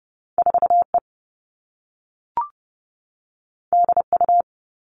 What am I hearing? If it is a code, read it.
Morse code, heard as a keyed tone, about 30 wpm, 700 Hz. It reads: 4 E D U
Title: Two Challenging Letter-Number Combinations 30wpm